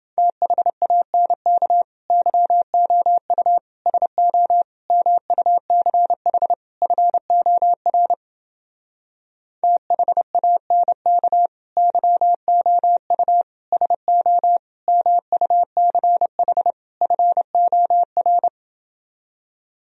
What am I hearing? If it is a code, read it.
Morse code, heard as a keyed tone, about 29 wpm, 700 Hz. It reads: THANK YOU SO MUCH FOR THANK YOU SO MUCH FOR